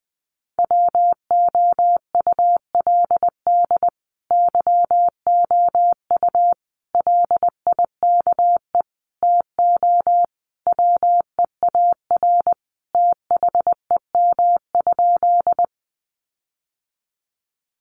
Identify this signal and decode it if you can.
Morse code, heard as a keyed tone, about 20 wpm, 700 Hz. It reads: WOULD YOU LIKE TO WEAR THEM?
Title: Sentences from Top 1500 Words 20wpm